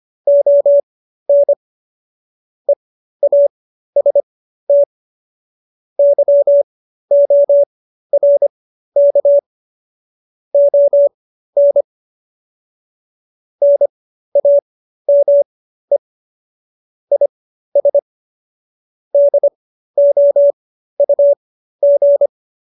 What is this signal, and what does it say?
Morse code, heard as a keyed tone, about 25 wpm, spaced as for 13 wpm, 570 Hz. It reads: ON EAST YORK ON NAME IS DOUG